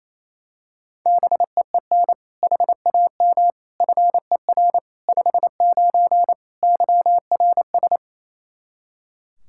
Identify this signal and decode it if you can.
Morse code, heard as a keyed tone, about 27 wpm, 700 Hz. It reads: BEEN HAM FER 59 YRS
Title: Practice QSO - Exchange 3 - More Info - 28wpm